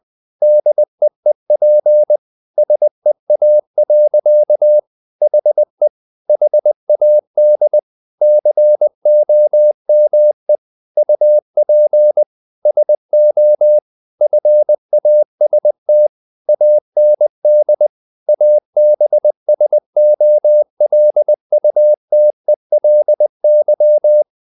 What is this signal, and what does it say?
Morse code, heard as a keyed tone, about 20 wpm, 600 Hz. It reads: DEEP SEA. HE HAD COME UP SO FAST AND ABSOLUTELY
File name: the old man and the sea part 2 20wpm